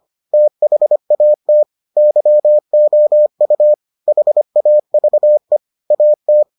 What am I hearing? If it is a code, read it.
Morse code, heard as a keyed tone, about 25 wpm, 600 Hz. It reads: THAT YOU HAVE AT